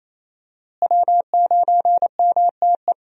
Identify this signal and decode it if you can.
Morse code, heard as a keyed tone, about 27 wpm, 700 Hz. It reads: W9MTE
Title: USA Call Signs - 5 Characters - Repeated 2x - 28wpm